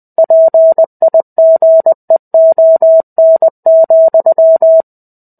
Morse code, heard as a keyed tone, about 20 wpm, 650 Hz. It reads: PIGEON,
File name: 05-Chapter-05 20 WPM